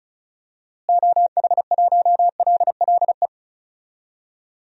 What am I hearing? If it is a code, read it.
Morse code, heard as a keyed tone, about 36 wpm, 700 Hz. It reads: OH1LLE
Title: DXCC-100 Call Signs - 6 Characters - Repeated 2x - 35wpm